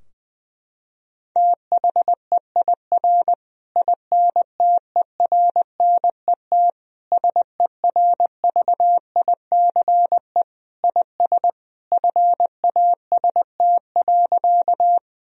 Morse code, heard as a keyed tone, about 20 wpm, 700 Hz. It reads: THEIR INTERNET SERVICE IS FAST.
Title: Sentences from Top 1000 Words - Review Part 4 - 20wpm